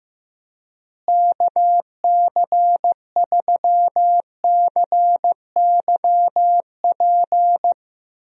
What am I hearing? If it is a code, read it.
Morse code, heard as a keyed tone, about 16 wpm, 700 Hz. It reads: KC3CYP
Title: USA Call Signs - 6 Characters - Repeated 3x - 15wpm